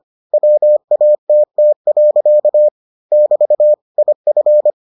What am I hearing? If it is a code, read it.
Morse code, heard as a keyed tone, about 25 wpm, 600 Hz. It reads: WATT. <BT> IF